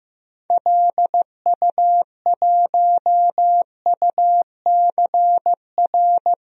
Morse code, heard as a keyed tone, about 15 wpm, 700 Hz. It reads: LU1UCR
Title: DXCC-100 Call Signs - 6 Characters - 15wpm